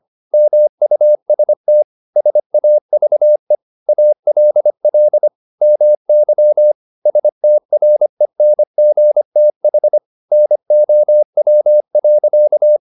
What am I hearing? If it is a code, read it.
Morse code, heard as a keyed tone, about 25 wpm, 600 Hz. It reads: MUST SAVE ALL MY STRENGTH NOW.
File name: the old man and the sea part 1 25wpm